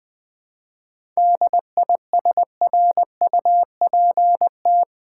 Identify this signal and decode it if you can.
Morse code, heard as a keyed tone, about 20 wpm, 700 Hz. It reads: DISRUPT
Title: 7 Letter Words - First 3 Letters Given - 20wpm